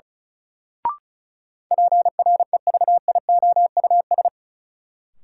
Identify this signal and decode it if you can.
Morse code, heard as a keyed tone, about 36 wpm, 700 Hz. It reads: PREVIOUS
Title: Sentences from Top 1500 Words - New Word Drill Alt - 35wpm